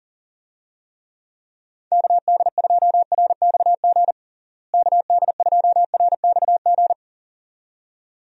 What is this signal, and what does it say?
Morse code, heard as a keyed tone, about 39 wpm, 700 Hz. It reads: KD2RXG KD2RXG
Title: USA Call Signs - 6 Characters - Repeated 2x - 40wpm